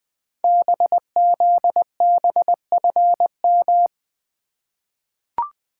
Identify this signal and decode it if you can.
Morse code, heard as a keyed tone, about 20 wpm, 700 Hz. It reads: BZBFM
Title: Code Groups - 5 Letters - 20wpm